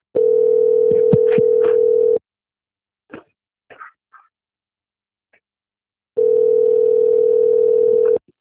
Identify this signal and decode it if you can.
Ringback tone (North America), 440 + 480 Hz.
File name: out- - - - - .54167